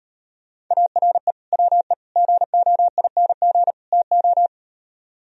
Morse code, heard as a keyed tone, about 38 wpm, 700 Hz. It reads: ARE WE GOING TO